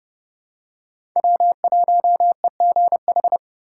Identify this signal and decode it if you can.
Morse code, heard as a keyed tone, about 30 wpm, 700 Hz. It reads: W1EGH
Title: USA Call Signs - 6x Character Spacing - 30wpm